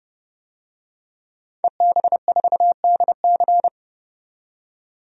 Morse code, heard as a keyed tone, about 30 wpm, 700 Hz. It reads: EB4DC
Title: DXCC-100 Call Signs - 5 Characters - 30wpm